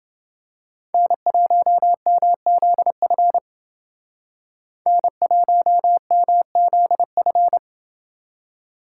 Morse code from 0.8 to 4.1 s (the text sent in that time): N1MZF